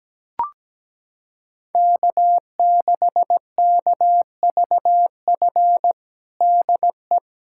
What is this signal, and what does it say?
Morse code, heard as a keyed tone, about 17 wpm, 700 Hz. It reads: K6KVF DE
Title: USA Callsign DE Callsign - 5 Characters - 17wpm